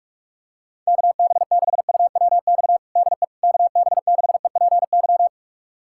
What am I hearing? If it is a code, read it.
Morse code, heard as a keyed tone, about 46 wpm, 700 Hz. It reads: KB6UWX DE KB6EPY
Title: USA Callsign DE Callsign 45wpm